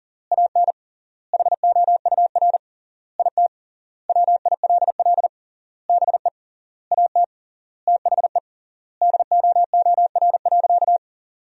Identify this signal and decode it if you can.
Morse code, heard as a keyed tone, about 37 wpm, 700 Hz. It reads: AN HOUR IT WILL BE AT THE DOOR.
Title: 3x Word-Spacing - Sentences from Top 400 Words - 40wpm